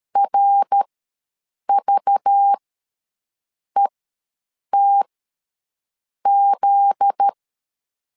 Morse code, heard as a keyed tone, about 13 wpm, spaced as for 6 wpm, 800 Hz. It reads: RVETZ
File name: LEZIONI / 8 YJ45 / 8 Y ws 12wpm 3409